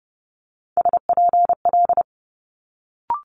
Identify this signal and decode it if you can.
Morse code, heard as a keyed tone, about 30 wpm, 700 Hz. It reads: SPL
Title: Multi-letter Phonemes Quick Fire 30wpm